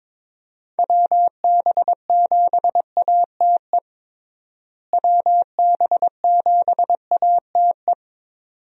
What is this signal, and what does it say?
Morse code, heard as a keyed tone, about 22 wpm, 700 Hz. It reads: WB7ATE WB7ATE
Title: USA Call Signs - 6 Characters - Repeated 2x - 22wpm